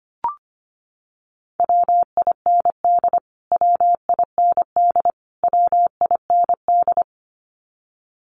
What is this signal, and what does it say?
Morse code, heard as a keyed tone, about 25 wpm, 700 Hz. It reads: WIND WIND WIND
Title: Top 500 Words Repeated 25wpm